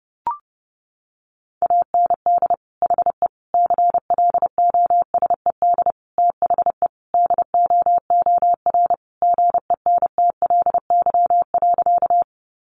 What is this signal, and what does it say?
Morse code, heard as a keyed tone, about 30 wpm, 700 Hz. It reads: AND HE CLOSED THE DOOR GENTLY.
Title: Sentences from Top 1500 Words 30wpm